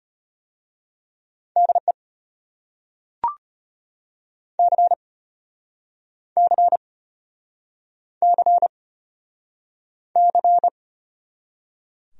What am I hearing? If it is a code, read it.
Morse code, heard as a keyed tone, about 33 wpm, 700 Hz. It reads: DE C C C C